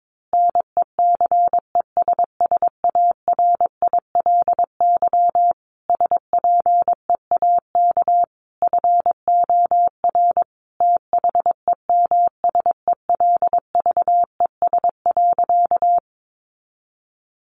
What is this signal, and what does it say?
Morse code, heard as a keyed tone, about 22 wpm, 700 Hz. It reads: NECESSARILY SPEAK FOR THEMSELVES.
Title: Sentences from Top 1400 Words 22wpm